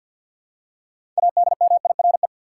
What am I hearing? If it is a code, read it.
Morse code, heard as a keyed tone, about 46 wpm, 700 Hz. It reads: ADMIRE
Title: Six Letter Words - 50wpm